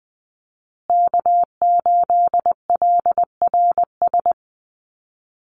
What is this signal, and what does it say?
Morse code, heard as a keyed tone, about 20 wpm, 700 Hz. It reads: K8LRS
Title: USA Call Signs Repeated 2x 20wpm